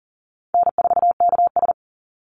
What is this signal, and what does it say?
Morse code, heard as a keyed tone, about 40 wpm, 700 Hz. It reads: N4KS
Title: Easy Call Signs Repeated 2x - 40wpm